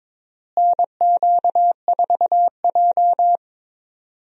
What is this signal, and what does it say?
Morse code, heard as a keyed tone, about 21 wpm, 700 Hz. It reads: NQ4J
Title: USA Call Signs - 4 Characters - Repeated 3x - 22wpm